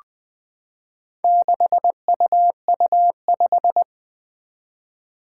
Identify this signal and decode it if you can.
Morse code, heard as a keyed tone, about 20 wpm, 700 Hz. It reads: 6UU5
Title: Four Challenging Letter-Number Combinations 20wpm